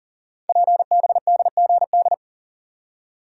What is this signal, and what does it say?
Morse code, heard as a keyed tone, about 42 wpm, 690 Hz. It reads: PBDGD
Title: Code Groups - 5 Letters - 40wpm